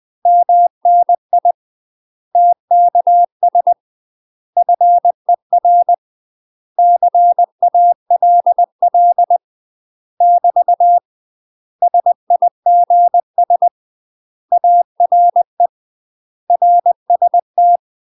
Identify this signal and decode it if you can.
Morse code, heard as a keyed tone, about 20 wpm, 700 Hz. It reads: MNI TKS FER CALL <BT> SIGS ARE RST
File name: UK MASTER QSOs TRAINING/20WPM